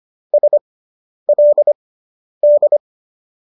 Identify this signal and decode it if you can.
Morse code, heard as a keyed tone, about 25 wpm, spaced as for 10 wpm, 590 Hz. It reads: SLD